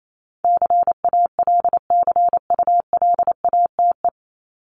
Morse code, heard as a keyed tone, about 28 wpm, 700 Hz. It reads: CALCULATE
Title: Top 1751-2000 Words 28wpm